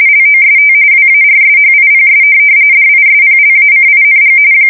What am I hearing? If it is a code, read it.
RTTY, about 45.45 baud. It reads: LOST WARSHIPS DISPLACING
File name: fsk170 / USS-New-Jersey